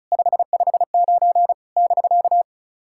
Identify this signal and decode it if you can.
Morse code, heard as a keyed tone, about 36 wpm, 700 Hz. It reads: 559 <BK>
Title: SOTA QSO Practice 35wpm